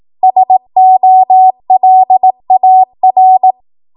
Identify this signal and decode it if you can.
Morse code, heard as a keyed tone, about 18 wpm, 750 Hz. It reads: SOLAR